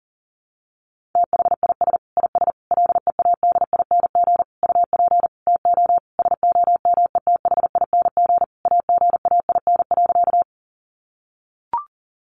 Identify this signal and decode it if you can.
Morse code, heard as a keyed tone, about 37 wpm, 700 Hz. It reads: THIS IS LEADING UP TO SOMETHING AGAIN.
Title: Sentences from Top 1100 Words 40wpm